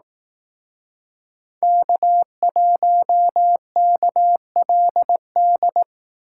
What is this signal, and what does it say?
Morse code, heard as a keyed tone, about 19 wpm, 700 Hz. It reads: K1KLD
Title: Speed Racing - USA 1x3 Call Signs 15wpm